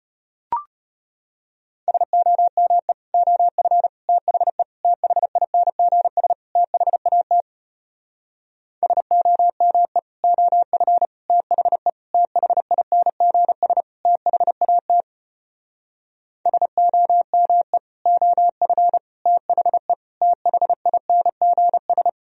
Morse code, heard as a keyed tone, about 33 wpm, 700 Hz. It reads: SOME OF THE THINGS THAT SOME OF THE THINGS THAT SOME OF THE THINGS